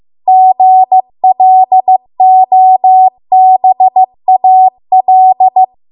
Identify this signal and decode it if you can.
Morse code, heard as a keyed tone, about 15 wpm, 750 Hz. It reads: GLOBAL
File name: Archive / 15 WPM / 15WPM